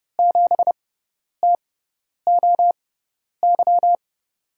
Morse code, heard as a keyed tone, about 30 wpm, spaced as for 10 wpm, 700 Hz. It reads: 7TOY
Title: USA Call Signs - 6x Character Spacing - 30wpm